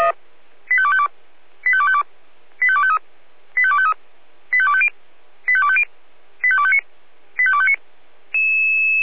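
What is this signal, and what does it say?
Five-tone selective call (ZVEI), heard as quick runs of five tones, tones of about 70 ms. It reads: 85252 85252 85253 85253 85250 85250 85259 85259
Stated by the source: ZVEI five-tone call received by SDR.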